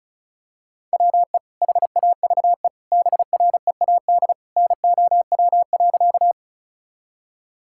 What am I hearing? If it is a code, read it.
Morse code, heard as a keyed tone, about 34 wpm, 700 Hz. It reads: WE HAVE BREAD NOW.
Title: Sentences from Top 1600 Words 35wpm